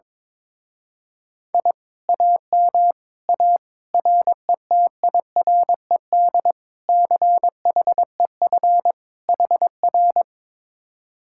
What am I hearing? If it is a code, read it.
Morse code, heard as a keyed tone, about 22 wpm, 700 Hz. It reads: I AM A RETIRED CHEF HR
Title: Speed-Racing Common QSO Phrases - 22wpm